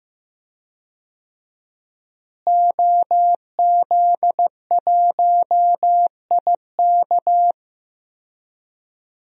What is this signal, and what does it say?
Morse code, heard as a keyed tone, about 15 wpm, 700 Hz. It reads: OZ1IK
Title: DXCC-100 Call Signs - 5 Characters - Repeated 2x - 15wpm